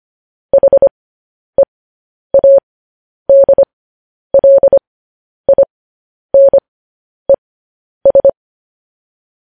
Morse code, heard as a keyed tone, about 25 wpm, spaced as for 10 wpm, 550 Hz. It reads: HEADLINES
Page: News Headlines in Morse Code at 10 WPM